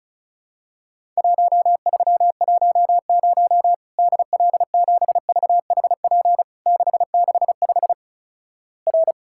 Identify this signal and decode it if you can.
Morse code, heard as a keyed tone, about 36 wpm, 700 Hz. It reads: 1310 DL7VHP 665 R
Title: Worked All Europe DX Contest - QTC Exchange - 35wpm